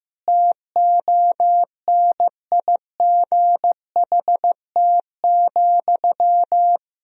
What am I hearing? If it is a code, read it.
Morse code, heard as a keyed tone, about 15 wpm, 700 Hz. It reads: TONIGHT,
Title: Sentences from Top 1000 Words - Review Part 3 - 15wpm